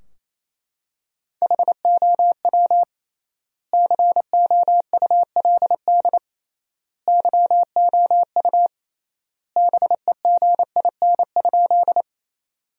Morse code, heard as a keyed tone, about 27 wpm, 700 Hz. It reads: HOW COULD YOU BEGIN?
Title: 3x Word-Spacing - Sentences from Top 300 Words - 28wpm